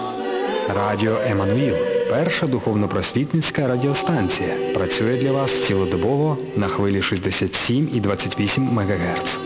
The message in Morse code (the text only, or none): none